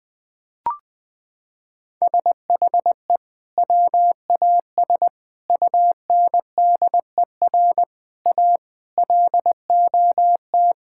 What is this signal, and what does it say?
Morse code, heard as a keyed tone, about 21 wpm, 700 Hz. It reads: SHE WAS UNDER A LOT
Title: Sentences from Top 1000 Words - Review Part 6 - 20wpm